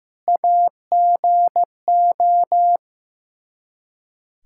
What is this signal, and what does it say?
Morse code, heard as a keyed tone, about 15 wpm, 700 Hz. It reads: AGO
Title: K1USN SST QSO Practice 15wpm